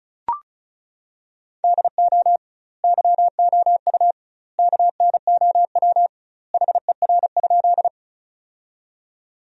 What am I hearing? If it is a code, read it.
Morse code, heard as a keyed tone, about 34 wpm, 700 Hz. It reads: DO YOU KNOW HER?
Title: 2x Word-Spacing - Sentences from Top 100 Words - 35wpm